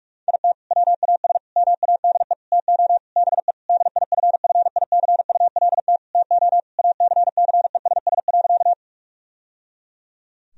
Morse code, heard as a keyed tone, about 46 wpm, 700 Hz. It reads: IT WAS MADE TO BE DIFFICULT TO ACCESS.
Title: Sentences from Top 1000 Words - Review Part 4 - 45wpm